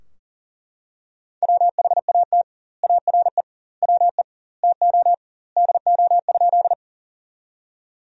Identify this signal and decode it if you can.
Morse code, heard as a keyed tone, about 40 wpm, 700 Hz. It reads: WHAT ARE WE TO DO?
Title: 2x Word-Spacing - Sentences from Top 100 Words - 40wpm